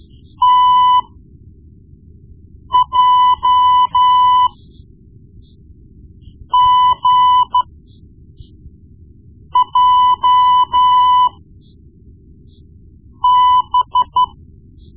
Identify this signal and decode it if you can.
Morse code, heard as a keyed tone, about 12 wpm, spaced as for 4 wpm, 1000 Hz. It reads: TJGJB
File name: Morse Lessons / Morse Lessons / Tape-1B